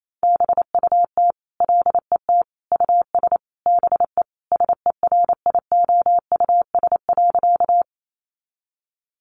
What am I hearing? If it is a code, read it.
Morse code, heard as a keyed tone, about 28 wpm, 700 Hz. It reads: BUT LET US BE SERIOUS.
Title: Sentences from Top 1100 Words 28wpm